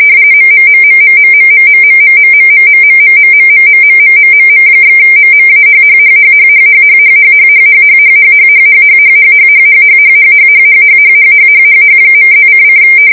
RTTY, about 45.45 baud. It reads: MMMMMMMMMMMMMMMMMMMMMMMMMM/MMAHHHHHHHHHHHHHHHHHHHHHHHHHHHHHHHHHHHHHH.+::